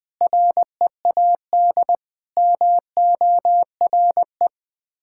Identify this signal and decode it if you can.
Morse code, heard as a keyed tone, about 20 wpm, 700 Hz. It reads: READ MORE
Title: Sets of 6 Words - 20wpm